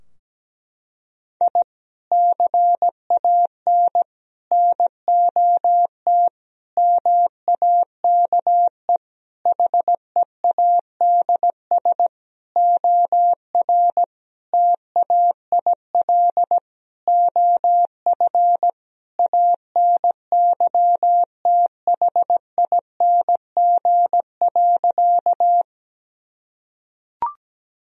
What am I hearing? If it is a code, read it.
Morse code, heard as a keyed tone, about 17 wpm, 700 Hz. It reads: I CAN NOT MAKE HEADS OR TAIL OF ANYTHING.
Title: Sentences from Top 1000 Words - Review Part 3 - 17wpm